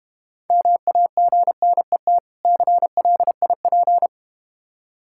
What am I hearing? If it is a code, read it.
Morse code, heard as a keyed tone, about 32 wpm, 690 Hz. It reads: MAGNET CLIP